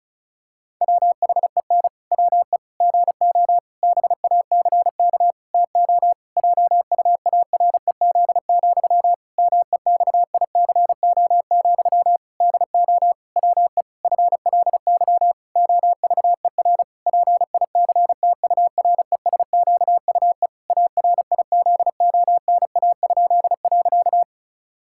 Morse code, heard as a keyed tone, about 34 wpm, 700 Hz. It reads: WHEN WE GO BACK TO JUAREZ, MEXICO, DO WE FLY OVER PICTURESQUE ARIZONA?.
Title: Pangram Challenge v2 - 35wpm